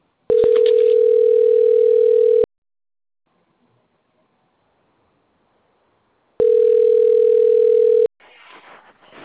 Ringback tone (North America), 440 + 480 Hz.
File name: out- - - - - .57355